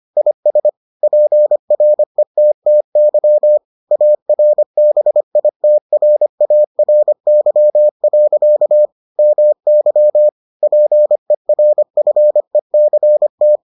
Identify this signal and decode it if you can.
Morse code, heard as a keyed tone, about 25 wpm, 600 Hz. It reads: IS PRETTY ARBITRARY. MY PERFECT